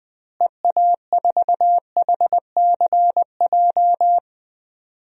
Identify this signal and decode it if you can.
Morse code, heard as a keyed tone, about 21 wpm, 700 Hz. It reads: EA4HCJ
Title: DXCC-100 Call Signs - 6 Characters - Repeated 2x - 20wpm